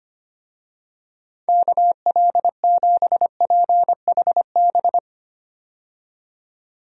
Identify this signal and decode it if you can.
Morse code, heard as a keyed tone, about 25 wpm, 700 Hz. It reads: KL7PHB
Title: DXCC-100 Call Signs - 6 Characters - 25wpm